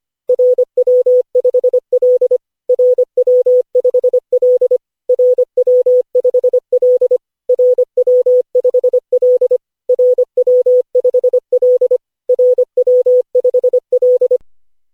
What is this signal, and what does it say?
Morse code, heard as a keyed tone, about 25 wpm, 500 Hz. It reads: RW5L RW5L RW5L RW5L RW5L RW5L
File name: cw / Session5 cw training